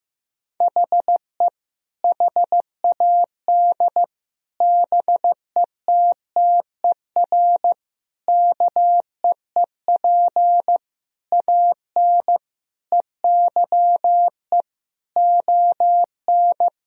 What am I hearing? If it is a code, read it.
Morse code, heard as a keyed tone, about 15 wpm, 700 Hz. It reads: HE HAD BETTER KEEP AN EYE ON
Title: Sentences from Top 1000 Words - Review Part 1 - 15wpm